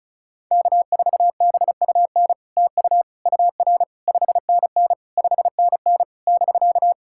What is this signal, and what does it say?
Morse code, heard as a keyed tone, about 36 wpm, 700 Hz. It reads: K4BUN TU UR 5NN 5NN <BK>
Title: SOTA QSO Practice 35wpm